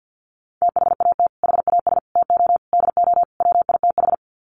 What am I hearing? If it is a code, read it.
Morse code, heard as a keyed tone, about 50 wpm, 700 Hz. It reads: THAT HAS TO DO WITH
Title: Sets of 5 Words Encore2 50wpm